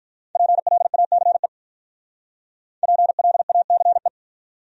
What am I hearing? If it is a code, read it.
Morse code, heard as a keyed tone, about 48 wpm, 700 Hz. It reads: PLACE PLACE